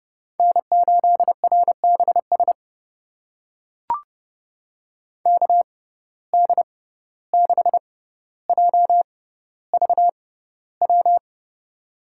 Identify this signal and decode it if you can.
Morse code, heard as a keyed tone, about 31 wpm, spaced as for 15 wpm, 700 Hz. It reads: N8RBS KD6JVW
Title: USA Call Signs - 6x Character Spacing - 30wpm